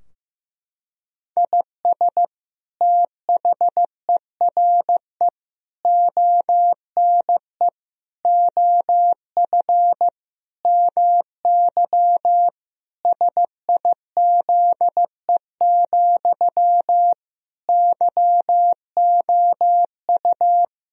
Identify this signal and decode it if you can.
Morse code, heard as a keyed tone, about 15 wpm, 700 Hz. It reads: IS THERE ONE OF MY SIZE, YOU
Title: Sentences from Top 1000 Words - Review Part 2 - 15wpm